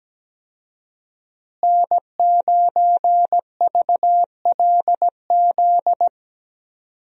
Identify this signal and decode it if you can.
Morse code, heard as a keyed tone, about 17 wpm, 700 Hz. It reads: N9VLZ
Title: USA Call Signs - 5 Characters - Repeated 2x - 17wpm